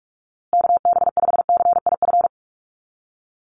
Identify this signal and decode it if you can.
Morse code, heard as a keyed tone, about 44 wpm, 700 Hz. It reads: KB5CIF
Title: USA Call Signs Repeated 3x 45wpm